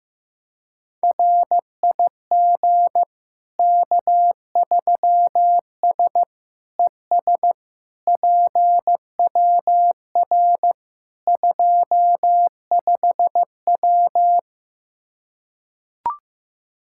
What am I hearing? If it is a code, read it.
Morse code, heard as a keyed tone, about 15 wpm, 700 Hz. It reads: RIG K3S ES PWR 25W
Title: Practice QSO - Exchange 2 - Info - 15wpm